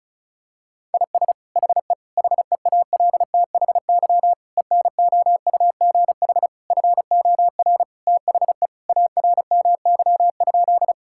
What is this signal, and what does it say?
Morse code, heard as a keyed tone, about 34 wpm, 700 Hz. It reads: IS HE HEALTHY ENOUGH FOR THE ARMY?
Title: Sentences from Top 1000 Words - Review Part 5 - 35wpm